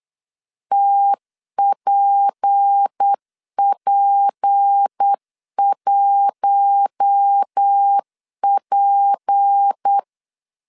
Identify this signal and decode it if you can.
Morse code, heard as a keyed tone, about 8 wpm, 800 Hz. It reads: TPP1P
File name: LEZIONI / 6 PKX0 / 6 P 8wpm 3025